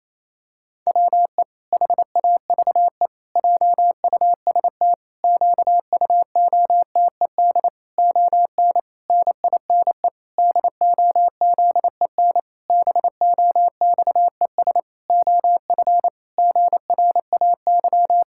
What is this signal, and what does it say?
Morse code, heard as a keyed tone, about 28 wpm, 700 Hz. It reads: WE HAVE JUST QUOTED ON NINE DOZEN BOXES OF GRAY
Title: Pangram Challenge v2 - 28wpm